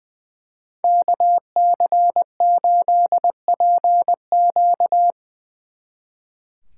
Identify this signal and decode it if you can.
Morse code, heard as a keyed tone, about 22 wpm, 700 Hz. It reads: KC8PQ
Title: Speed Racing - USA Call Signs 20wpm